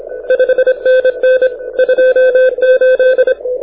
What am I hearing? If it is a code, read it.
Morse code, heard as a keyed tone, about 27 wpm, 1550 Hz. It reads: 5NN 28
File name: audio zawody / IARU 2015 CW SN7Q